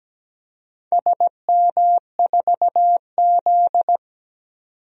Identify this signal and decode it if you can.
Morse code, heard as a keyed tone, about 17 wpm, 700 Hz. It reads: SM4Z
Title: DXCC-100 Call Signs - 4 Characters - 17wpm